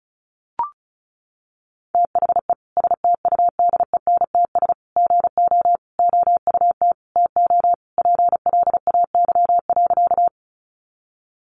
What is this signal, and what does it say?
Morse code, heard as a keyed tone, about 34 wpm, 700 Hz. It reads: THE STUDENTS GO OUT TO PLAY.
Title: Easy Phrases 35wpm